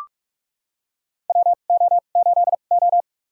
Morse code, heard as a keyed tone, about 45 wpm, spaced as for 32 wpm, 690 Hz. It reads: WO8O
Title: USA Call Signs - 4 Characters - 2x Character Spacing - 45wpm